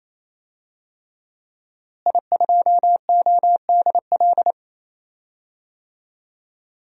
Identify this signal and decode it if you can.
Morse code, heard as a keyed tone, about 27 wpm, 700 Hz. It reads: I2ODL
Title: DXCC-100 Call Signs - 5 Characters - Repeated 3x - 28wpm